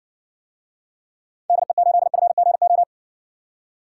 Morse code, heard as a keyed tone, about 60 wpm, 700 Hz. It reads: DE8FGO